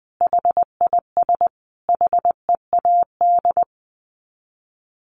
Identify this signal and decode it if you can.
Morse code, heard as a keyed tone, about 20 wpm, 700 Hz. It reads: HIS HEAD
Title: Sets of 5 Words Encore 20wpm